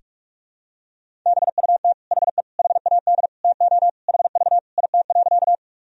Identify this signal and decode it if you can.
Morse code, heard as a keyed tone, about 41 wpm, 700 Hz. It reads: BUT HE HAD TO HV IT.
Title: Sentences from Top 100 Words and Abbreviations 45wpm